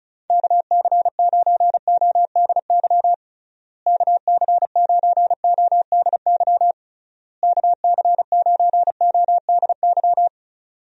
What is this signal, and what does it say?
Morse code, heard as a keyed tone, about 36 wpm, 700 Hz. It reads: KC9ODY KC9ODY KC9ODY
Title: USA Call Signs - 6 Characters - Repeated 3x - 35wpm